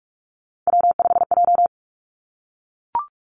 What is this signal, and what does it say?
Morse code, heard as a keyed tone, about 46 wpm, 700 Hz. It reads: W5J
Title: Three Challenging Letter-Number Combinations 45wpm